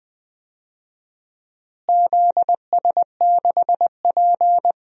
Morse code, heard as a keyed tone, about 20 wpm, 700 Hz. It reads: ZS6P
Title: DXCC-100 Call Signs - 4 Characters - Repeated 3x - 20wpm